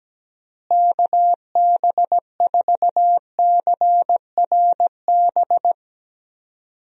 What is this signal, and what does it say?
Morse code, heard as a keyed tone, about 17 wpm, 700 Hz. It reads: KB4CRB
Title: USA Call Signs - 6 Characters - Repeated 2x - 17wpm